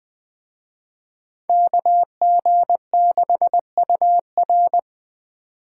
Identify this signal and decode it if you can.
Morse code, heard as a keyed tone, about 20 wpm, 700 Hz. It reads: KG6UR
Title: Speed Racing - USA Call Signs 17wpm